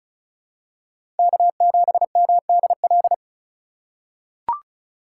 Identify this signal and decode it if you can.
Morse code, heard as a keyed tone, about 36 wpm, 700 Hz. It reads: K7MDL
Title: USA Call Signs - 5 Characters - 2x Character Spacing - 35wpm